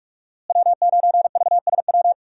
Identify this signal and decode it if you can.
Morse code, heard as a keyed tone, about 41 wpm, 700 Hz. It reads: W9VSW
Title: USA Call Signs - 5 Characters - Repeated 3x - 45wpm